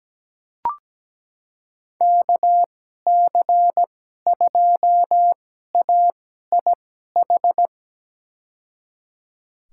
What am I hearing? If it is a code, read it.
Morse code, heard as a keyed tone, about 17 wpm, spaced as for 12 wpm, 700 Hz. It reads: KC2AIH
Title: USA Call Signs - 6 Characters - 2x Character Spacing - 17wpm